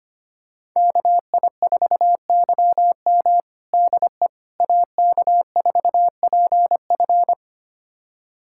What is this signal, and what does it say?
Morse code, heard as a keyed tone, about 25 wpm, 700 Hz. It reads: KI4YM DE AK4PF
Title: USA Callsign DE Callsign - 5 Characters - 25wpm